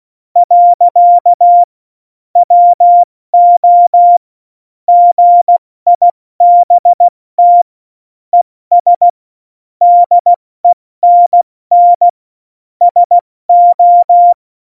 Morse code, heard as a keyed tone, about 16 wpm, 700 Hz. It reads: . WO GIBT ES DENN SO
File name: Morsekurs / Witze / Witz 19. Juni 2018, 70 F80